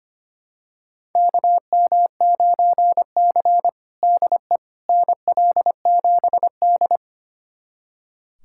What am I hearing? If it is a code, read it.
Morse code, heard as a keyed tone, about 25 wpm, 700 Hz. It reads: KM9C DE NL7D
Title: USA Callsign DE Callsign - 4 Characters - 25wpm